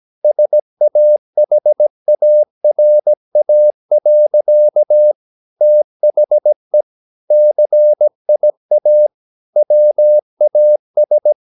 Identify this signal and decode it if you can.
Morse code, heard as a keyed tone, about 17 wpm, 600 Hz. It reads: SAHARA. THE CIA WAS
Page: Romanen Never af Ken Follet på engelsk.